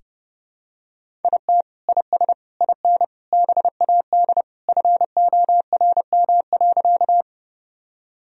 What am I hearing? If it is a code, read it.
Morse code, heard as a keyed tone, about 30 wpm, 700 Hz. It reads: IT IS IN BAD FORM.
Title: Sentences from Top 1000 Words - Review Part 1 - 30wpm